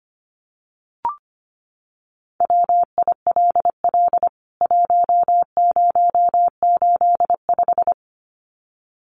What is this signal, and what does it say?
Morse code, heard as a keyed tone, about 25 wpm, 690 Hz. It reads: WILL 1085
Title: CWT Exchange 25wpm